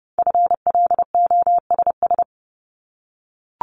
Morse code, heard as a keyed tone, about 30 wpm, 700 Hz. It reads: FLOSS
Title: Multi-letter Phonemes Repeated 3x 30wpm